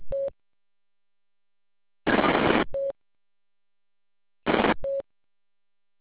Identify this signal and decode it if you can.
There is no code.